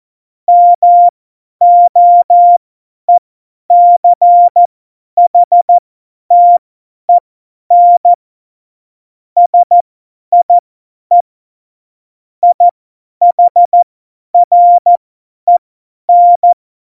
Morse code, heard as a keyed tone, about 14 wpm, spaced as for 10 wpm, 700 Hz. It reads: MOECHTEN SIE IHREN
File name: Morsekurs / Witze / Witz 10. September 2019, 50 F70